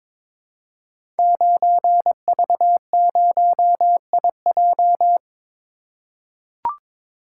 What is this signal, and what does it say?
Morse code, heard as a keyed tone, about 23 wpm, 700 Hz. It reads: 9V0IJ
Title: Code Groups - 5 Letter:Numbers - 22wpm